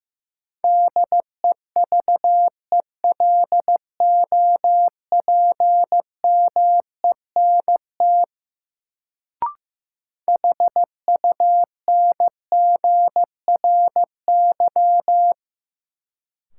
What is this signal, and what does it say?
Morse code, heard as a keyed tone, about 15 wpm, 700 Hz. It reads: DEVELOPMENT HUNGRY
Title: Sentences from Top 1400 Words - New Word Drill Alt - 15wpm